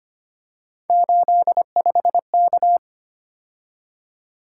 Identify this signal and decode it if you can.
Morse code, heard as a keyed tone, about 25 wpm, 700 Hz. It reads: 85K
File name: Three Letter-Number Combinations - 25wpm